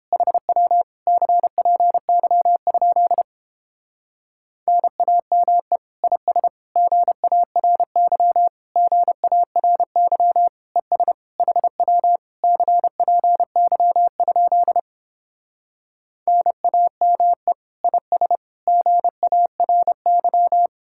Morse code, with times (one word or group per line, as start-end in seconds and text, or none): 0.1-0.8 HW
1.1-3.2 CPY?
4.7-5.8 NAME
6.0-6.5 IS
6.8-8.5 GARY
8.8-10.5 GARY
10.8-11.1 ES
11.4-12.2 HW
12.4-14.8 CPY?
16.3-17.5 NAME
17.8-18.4 IS
18.7-20.7 GARY